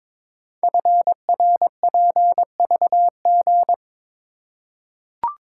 Morse code, heard as a keyed tone, about 22 wpm, 700 Hz. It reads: FRPVG
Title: Code Groups - 5 Letters - 22wpm